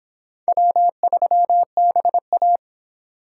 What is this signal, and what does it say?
Morse code, heard as a keyed tone, about 29 wpm, 700 Hz. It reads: W3BA